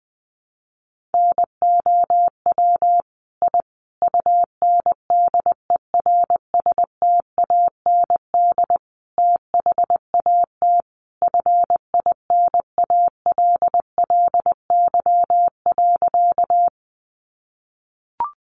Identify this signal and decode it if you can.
Morse code, heard as a keyed tone, about 20 wpm, 700 Hz. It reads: NOW I UNDERSTAND THAT FINALLY.
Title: Sentences from Top 1000 Words 20wpm